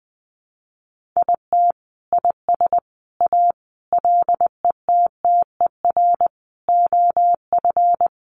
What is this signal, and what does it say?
Morse code, heard as a keyed tone, about 20 wpm, 700 Hz. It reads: IT IS A LETTER OF
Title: Sentences from Top 1400 Words 20wpm